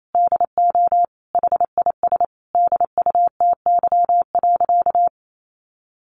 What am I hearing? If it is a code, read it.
Morse code, heard as a keyed tone, about 28 wpm, 700 Hz. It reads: DO HIS DUTY.
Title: Sentences from Top 1500 Words 28wpm